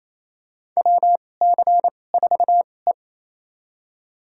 Morse code, heard as a keyed tone, about 27 wpm, spaced as for 20 wpm, 700 Hz. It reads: WC4E
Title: USA Call Signs - 4 Characters - 2x Character Spacing - 28wpm